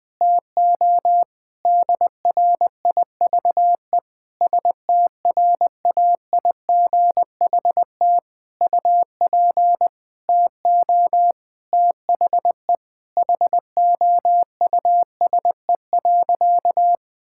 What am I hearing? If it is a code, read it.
Morse code, heard as a keyed tone, about 20 wpm, 700 Hz. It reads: TO DRIVE STRAIGHT UP TO THE HOUSE.
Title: Sentences from Top 1000 Words - Review Part 6 - 20wpm